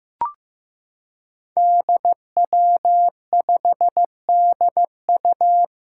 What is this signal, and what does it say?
Morse code, heard as a keyed tone, about 15 wpm, 700 Hz. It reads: DW5DU
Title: Five Challenging Letter-Number Combinations 15wpm